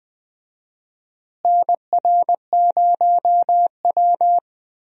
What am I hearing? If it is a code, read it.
Morse code, heard as a keyed tone, about 20 wpm, 700 Hz. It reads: NR0W
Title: USA Call Signs - 4 Characters - Repeated 2x - 20wpm